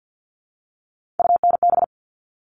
Morse code, heard as a keyed tone, about 50 wpm, 700 Hz. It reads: UNB
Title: All Cut Numbers - 3-Digit 50wpm